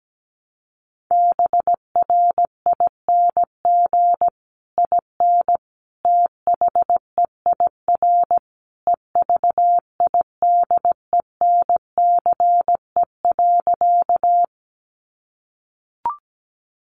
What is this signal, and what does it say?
Morse code, heard as a keyed tone, about 17 wpm, 700 Hz. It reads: BRING IN THEIR EVIDENCE.
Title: Sentences from Top 1300 Words 17wpm